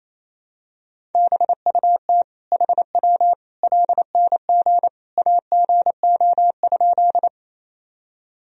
Morse code, heard as a keyed tone, about 28 wpm, 700 Hz. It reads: BUT HW LNG AGO?
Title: Sentences from Top 500 Words and Abbreviations 28wpm